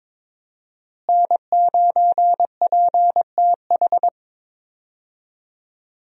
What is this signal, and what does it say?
Morse code, heard as a keyed tone, about 21 wpm, 700 Hz. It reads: N9PTH
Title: USA Call Signs - 5 Characters - Repeated 2x - 22wpm